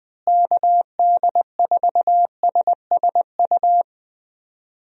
Morse code, heard as a keyed tone, about 20 wpm, 700 Hz. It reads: KD4SSU
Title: USA Call Signs - 6 Characters - Repeated 3x - 20wpm